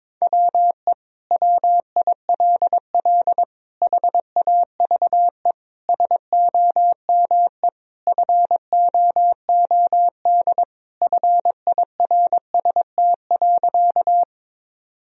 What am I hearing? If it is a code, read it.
Morse code, heard as a keyed tone, about 22 wpm, 700 Hz. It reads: WE WILL HAVE SOME FOOD FIRST.
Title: Sentences from Top 1000 Words - Review Part 1 - 22wpm